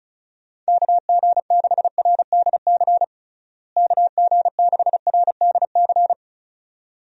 Morse code, heard as a keyed tone, about 36 wpm, 700 Hz. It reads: KG6RDC KG6RDC
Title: USA Call Signs - 6 Characters - Repeated 2x - 35wpm